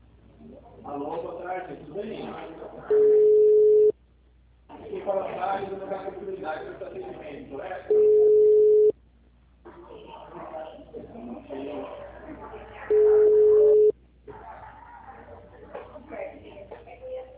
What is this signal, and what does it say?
Ringback tone (Europe, 425 Hz), with speech.